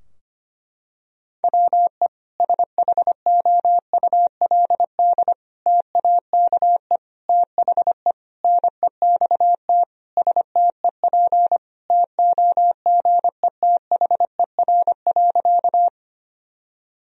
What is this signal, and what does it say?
Morse code, heard as a keyed tone, about 25 wpm, 700 Hz. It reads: WE SHOULD TAKE THE NEXT STEP TOGETHER.
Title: Sentences from Top 1000 Words - Review Part 2 - 25wpm